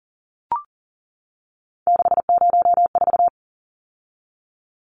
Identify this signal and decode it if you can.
Morse code, heard as a keyed tone, about 40 wpm, 700 Hz. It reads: 604
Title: Numbers - 3-Digit 40wpm